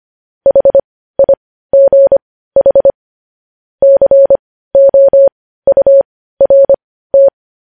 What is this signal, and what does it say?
Morse code, heard as a keyed tone, about 25 wpm, spaced as for 15 wpm, 550 Hz. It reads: HIGH COURT
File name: podcast / News headlines-1x-25x15